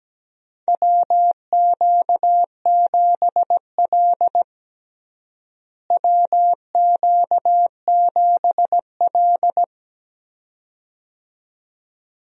Morse code, heard as a keyed tone, about 17 wpm, 700 Hz. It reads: WQ7L WQ7L
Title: USA Call Signs - 4 Characters - Repeated 2x - 17wpm